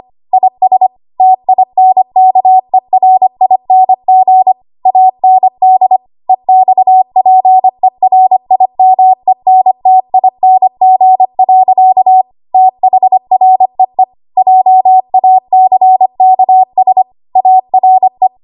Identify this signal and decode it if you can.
Morse code, heard as a keyed tone, about 25 wpm, 750 Hz. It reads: IS TINKERING AND EXPERIMENTING. THREE JACKS ARE
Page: 25 WPM Code Archive